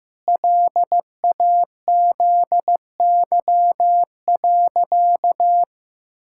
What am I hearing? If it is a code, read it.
Morse code, heard as a keyed tone, about 15 wpm, 700 Hz. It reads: LAZY.
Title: Pangram Challenge v2 - 15wpm